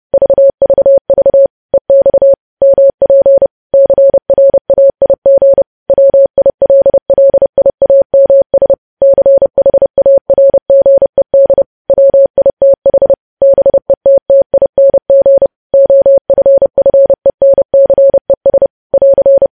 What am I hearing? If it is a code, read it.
Morse code, heard as a keyed tone, about 30 wpm, 550 Hz. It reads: VVV EX MP CRAIG WILLIAMS CHARGED WITH BETTING OFFENCES <AR>
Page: news headlines in morse code at 30 wpm